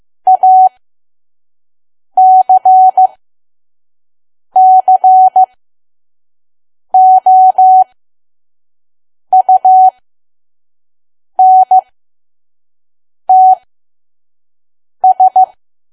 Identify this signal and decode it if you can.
Morse code, heard as a keyed tone, about 15 wpm, spaced as for 5 wpm, 750 Hz. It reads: ACCOUNTS
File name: Archive / 5 WPM files / PM